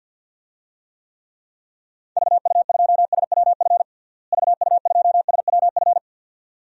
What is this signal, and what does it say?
Morse code, heard as a keyed tone, about 53 wpm, 700 Hz. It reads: VU2SWF VU2SWF
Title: DXCC-100 Call Signs - 6 Characters - Repeated 2x - 50wpm